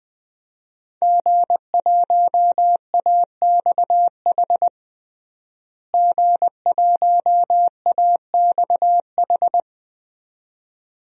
Morse code, heard as a keyed tone, about 20 wpm, 700 Hz. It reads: G1AXH G1AXH
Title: DXCC-100 Call Signs - 5 Characters - Repeated 2x - 20wpm